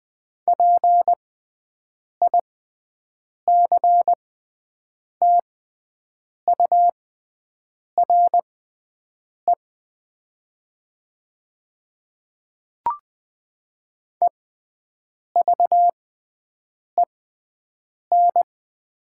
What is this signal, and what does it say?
Morse code, heard as a keyed tone, about 20 wpm, spaced as for 7 wpm, 700 Hz. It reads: PICTURE EVEN
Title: 6x Character Spacing - Top 500 Words - 20wpm